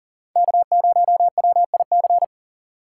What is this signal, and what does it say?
Morse code, heard as a keyed tone, about 40 wpm, 700 Hz. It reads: K0WIC
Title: USA Call Signs - 5 Characters - 4x Character Spacing - 40wpm